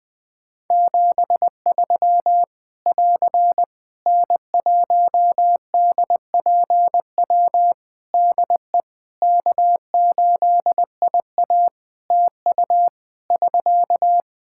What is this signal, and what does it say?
Morse code, heard as a keyed tone, about 21 wpm, 700 Hz. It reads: 73 <AR> N1DPW DE K8IA TU <SK>
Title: K1USN SST QSO Practice 20wpm